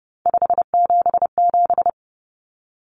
Morse code, heard as a keyed tone, about 30 wpm, 700 Hz. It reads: 577
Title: Common Cut Numbers - 3-Digit 30wpm